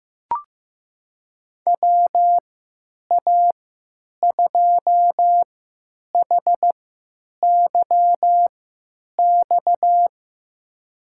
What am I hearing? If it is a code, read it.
Morse code, heard as a keyed tone, about 15 wpm, spaced as for 9 wpm, 700 Hz. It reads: WA2HYX
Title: USA Call Signs - 6 Characters - 3x Character Spacing - 15wpm